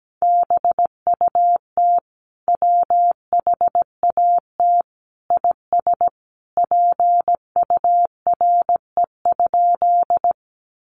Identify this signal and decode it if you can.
Morse code, heard as a keyed tone, about 17 wpm, 700 Hz. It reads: BUT WHAT IS PURE?
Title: Sentences from Top 1500 Words 17wpm